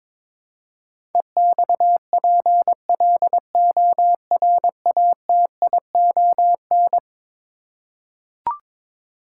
Morse code, heard as a keyed tone, about 21 wpm, 700 Hz. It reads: EXPLORATION
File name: practice / 22wpm / Eleven Letter Words - 22wpm